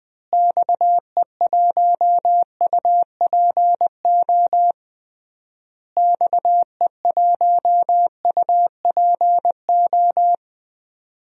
Morse code, heard as a keyed tone, about 19 wpm, 700 Hz. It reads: XE1UPO XE1UPO
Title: DXCC-100 Call Signs - 6 Characters - Repeated 3x - 20wpm